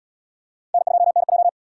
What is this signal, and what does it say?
Morse code, heard as a keyed tone, about 74 wpm, 700 Hz. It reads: N2NJ